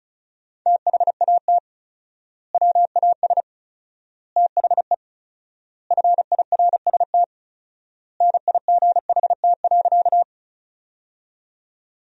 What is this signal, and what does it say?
Morse code, heard as a keyed tone, about 34 wpm, 700 Hz. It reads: THAT WAS THE FIRST NIGHT.
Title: 4x Word-Spacing - Sentences from Top 300 Words - 35wpm